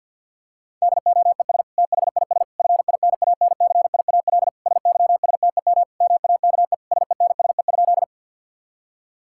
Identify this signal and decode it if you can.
Morse code, heard as a keyed tone, about 50 wpm, 700 Hz. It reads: DOES THIS FINANCIAL SYSTEM MAKE SENSE?
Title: Sentences from Top 1000 Words - Review Part 5 - 50wpm